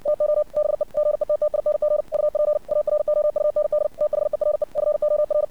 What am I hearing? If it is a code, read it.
Morse code, heard as a keyed tone, about 60 wpm, 600 Hz. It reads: TO BE GETTING UP AROUND THERE FOR
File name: archived / CRSnet